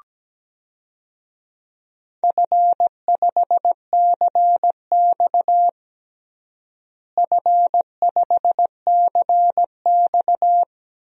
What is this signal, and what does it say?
Morse code, heard as a keyed tone, about 17 wpm, 700 Hz. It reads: F5CX F5CX
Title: DXCC-100 Call Signs - 4 Characters - Repeated 2x - 17wpm